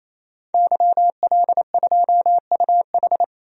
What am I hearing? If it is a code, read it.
Morse code, heard as a keyed tone, about 28 wpm, 700 Hz. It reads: YL2UH
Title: DXCC-100 Call Signs - 5 Characters - 28wpm